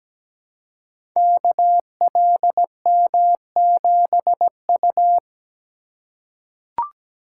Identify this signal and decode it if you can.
Morse code, heard as a keyed tone, about 17 wpm, 700 Hz. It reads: KLM7U
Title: Code Groups - 5 Letter:Numbers - 17wpm